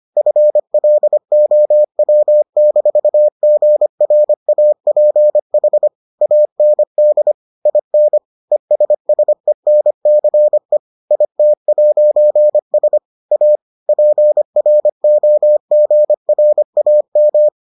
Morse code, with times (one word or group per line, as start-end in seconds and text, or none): 0.2-5.9 FLOW-GRAPH
6.2-7.3 AND
7.7-8.2 IN
8.5-10.8 ESSENCE
11.1-13.0 IT'S
13.3-13.6 A
13.9-17.5 PROGRAM